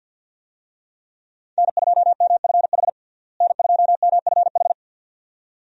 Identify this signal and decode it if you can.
Morse code, heard as a keyed tone, about 51 wpm, 700 Hz. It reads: N2MFH N2MFH
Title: USA Call Signs - 5 Characters - Repeated 2x - 50wpm